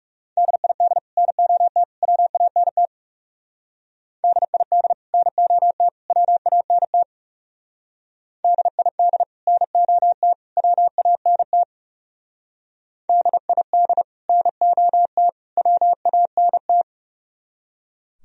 Morse code, heard as a keyed tone, about 37 wpm, 700 Hz. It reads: DID NOT WANT DID NOT WANT DID NOT WANT DID NOT WANT
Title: Speed-Racing - Sets of 3 Words 30wpm